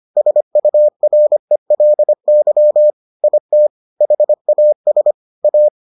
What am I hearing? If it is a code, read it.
Morse code, heard as a keyed tone, about 25 wpm, 600 Hz. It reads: SURELY IT HAS A